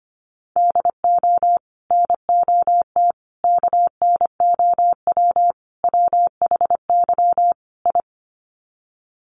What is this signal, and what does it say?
Morse code, heard as a keyed tone, about 25 wpm, 700 Hz. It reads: DO NOT KNOW WHY I
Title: Sets of 5 Words Encore 25wpm